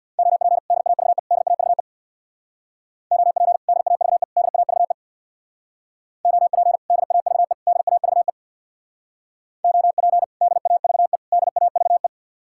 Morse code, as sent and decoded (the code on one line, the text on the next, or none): --- .--. / -.. .- ...- . / -.. .- ...- . / --- .--. / -.. .- ...- . / -.. .- ...- . / --- .--. / -.. .- ...- . / -.. .- ...- . / --- .--. / -.. .- ...- . / -.. .- ...- .
OP DAVE DAVE OP DAVE DAVE OP DAVE DAVE OP DAVE DAVE